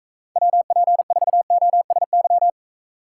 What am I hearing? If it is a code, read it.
Morse code, heard as a keyed tone, about 44 wpm, 700 Hz. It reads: WP4OSY